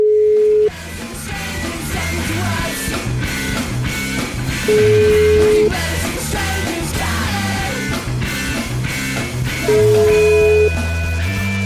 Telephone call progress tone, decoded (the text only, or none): ringback tone